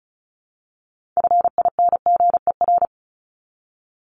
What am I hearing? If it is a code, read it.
Morse code, heard as a keyed tone, about 34 wpm, 700 Hz. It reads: FINGER
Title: Top 1501-1750 Words 35wpm